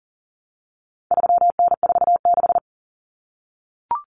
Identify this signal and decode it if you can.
Morse code, heard as a keyed tone, about 40 wpm, 700 Hz. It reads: 3N46
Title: Common Cut Numbers - 4-Digit 40wpm